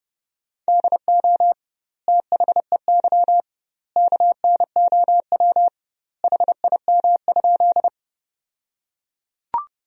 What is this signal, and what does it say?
Morse code, heard as a keyed tone, about 30 wpm, 700 Hz. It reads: DO THEY KNOW HIM?
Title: 2x Word-Spacing - Sentences from Top 100 Words - 30wpm